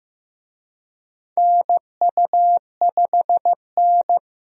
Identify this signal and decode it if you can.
Morse code, heard as a keyed tone, about 16 wpm, 700 Hz. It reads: NU5N
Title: USA Call Signs - 4 Characters - Repeated 3x - 15wpm